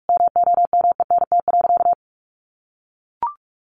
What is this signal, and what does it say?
Morse code, heard as a keyed tone, about 44 wpm, 700 Hz. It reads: MOMENT.
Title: Sentences from Top 1500 Words 45wpm